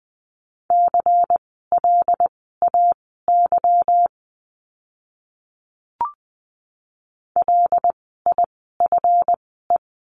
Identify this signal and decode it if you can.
Morse code, heard as a keyed tone, about 20 wpm, spaced as for 15 wpm, 700 Hz. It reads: CLAY LIFE
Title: 146 Lesson 24f - Letter y - Words 20wpm